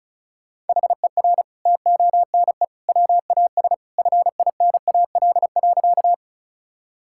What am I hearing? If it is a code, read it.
Morse code, heard as a keyed tone, about 36 wpm, 700 Hz. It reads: HER TONE WAS FINAL.
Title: Sentences from Top 1600 Words 35wpm